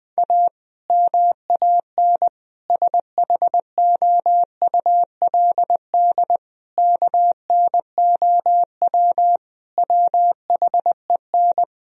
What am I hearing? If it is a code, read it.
Morse code, heard as a keyed tone, about 21 wpm, 700 Hz. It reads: A MAN SHOULD KNOW WHEN
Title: Sentences from Top 1000 Words - Review Part 2 - 20wpm